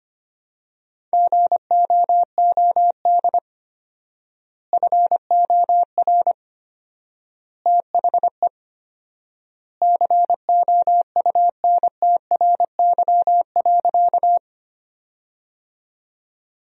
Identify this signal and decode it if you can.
Morse code, heard as a keyed tone, about 25 wpm, 700 Hz. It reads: GOOD FOR THE COUNTRY.
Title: 4x Word-Spacing - Sentences from Top 300 Words - 25wpm